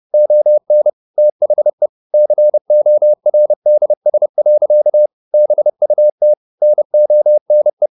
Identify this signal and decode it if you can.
Morse code, heard as a keyed tone, about 30 wpm, 600 Hz. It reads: ON THE CORDS. BUT NONE
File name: the old man and the sea / the old man and the sea 1 30wpm